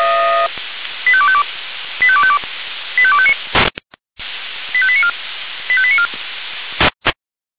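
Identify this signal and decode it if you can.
Five-tone selective call (ZVEI), heard as quick runs of five tones, tones of about 70 ms. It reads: 85252 85252 85259 85894 85894
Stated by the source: ZVEI five-tone call 85259 received via SDR.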